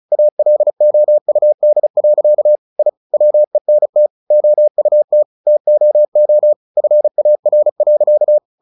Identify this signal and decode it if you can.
Morse code, heard as a keyed tone, about 34 wpm, 600 Hz. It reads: ALOUD. I WENT OUT TOO FAR.
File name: the old man and the sea 2 35wpm